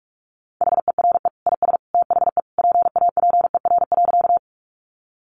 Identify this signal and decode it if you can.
Morse code, heard as a keyed tone, about 44 wpm, 700 Hz. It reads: HERE IS THE PAPER.
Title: Sentences from Top 300 Words 45wpm